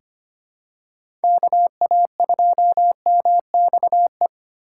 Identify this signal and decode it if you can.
Morse code, heard as a keyed tone, about 25 wpm, 700 Hz. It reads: KA2MXE
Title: USA Call Signs - 6 Characters - 2x Character Spacing - 25wpm